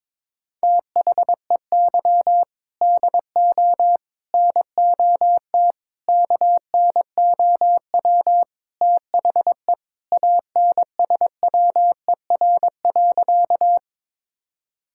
Morse code, heard as a keyed tone, about 22 wpm, 700 Hz. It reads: THEY DO NOT KNOW THE ANSWER.
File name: practice / 22wpm / Sentences with 4-Word Phrases - 22wpm